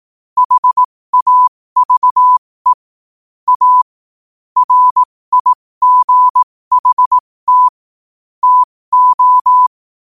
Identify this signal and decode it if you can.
Morse code, heard as a keyed tone, about 18 wpm, spaced as for 15 wpm, 1000 Hz. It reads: HAVE A RIGHT TO
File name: qotdfiles / 15wpm